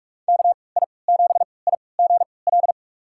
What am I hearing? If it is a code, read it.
Morse code, heard as a keyed tone, about 46 wpm, spaced as for 26 wpm, 700 Hz. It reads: KI7IGL
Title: USA Call Signs - 6 Characters - 3x Character Spacing - 45wpm